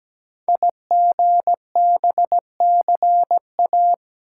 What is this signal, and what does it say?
Morse code, heard as a keyed tone, about 17 wpm, 700 Hz. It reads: IGBCA
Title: Code Groups - 5 Letters - 17wpm